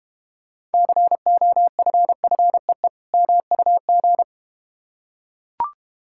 Morse code, heard as a keyed tone, about 31 wpm, 700 Hz. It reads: COFFEE MUG